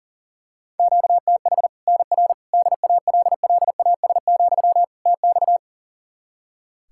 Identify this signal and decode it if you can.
Morse code, heard as a keyed tone, about 40 wpm, 700 Hz. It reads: QTH NR DALLAS, TX
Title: Practice QSO - Exchange 1 - Beginning - 40wpm